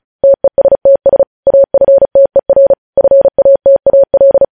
Morse code, heard as a keyed tone, about 36 wpm, 550 Hz. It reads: TESTS AFTER FATAL
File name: podcast / News headlines-1x-35x35